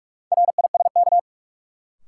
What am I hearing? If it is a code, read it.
Morse code, heard as a keyed tone, about 46 wpm, 700 Hz. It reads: RISK
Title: Sentences from Top 600 Words - New Word Drill - 45wpm